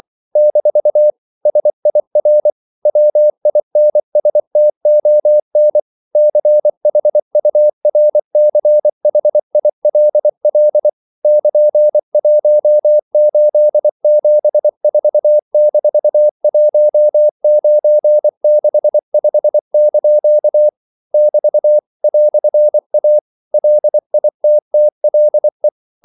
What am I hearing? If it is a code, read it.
Morse code, heard as a keyed tone, about 24 wpm, 600 Hz. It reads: - SIR WINSTON CHURCHILL <KN>1874-1965) <BT> "A LITTLE
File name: CW / Practice / EnglishQuotations0224